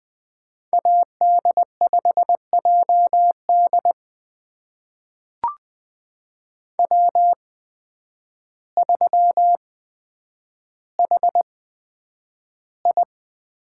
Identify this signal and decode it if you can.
Morse code, heard as a keyed tone, about 20 wpm, spaced as for 9 wpm, 700 Hz. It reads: AD5JD W3HI
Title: USA Call Signs - 5 Characters - 8x Character Spacing - 20wpm